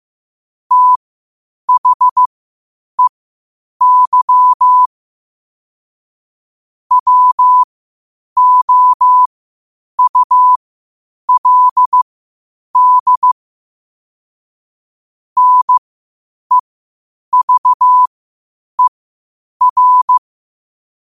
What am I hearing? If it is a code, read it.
Morse code, heard as a keyed tone, about 15 wpm, spaced as for 8 wpm, 1000 Hz. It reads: THEY WOULD NEVER
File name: qotdfiles / 8wpm